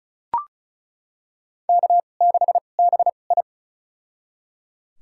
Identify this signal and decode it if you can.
Morse code, heard as a keyed tone, about 34 wpm, spaced as for 25 wpm, 700 Hz. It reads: K6BI
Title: USA Call Signs - 4 Characters - 2x Character Spacing - 35wpm